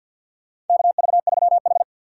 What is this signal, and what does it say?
Morse code, heard as a keyed tone, about 47 wpm, 700 Hz. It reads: KV3H
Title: USA Call Signs - 4x Character Spacing - 50wpm